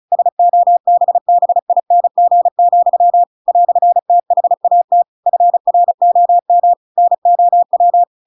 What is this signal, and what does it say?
Morse code, heard as a keyed tone, about 34 wpm, 700 Hz. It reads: SOBBING, "THAT FROM NOW
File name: Pinocchio / Pinocchio ch8 35 35 ik1hge